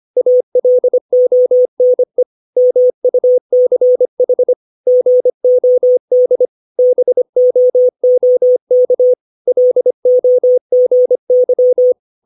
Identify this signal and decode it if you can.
Morse code, heard as a keyed tone, about 25 wpm, 500 Hz. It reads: ALONE MUCH GOD BOOK LOGY
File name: 632 words / 632 Words 25WPM